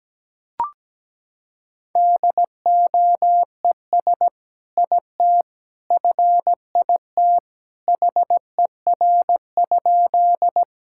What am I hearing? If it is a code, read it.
Morse code, heard as a keyed tone, about 17 wpm, 700 Hz. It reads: DOES IT FIT HER?
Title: Sentences from Top 1000 Words - Review Part 5 - 17wpm